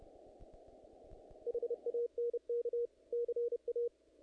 Morse code, heard as a keyed tone, about 28 wpm, 480 Hz. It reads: HANK CA
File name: contests / naqp-cw-k5tr-op-k5pi- -1900